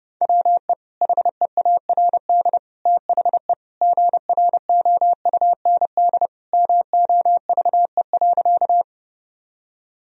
Morse code, heard as a keyed tone, about 30 wpm, 700 Hz. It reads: WE HEARD THE GROUND MOVE.
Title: Sentences from Top 1000 Words - Review Part 2 - 30wpm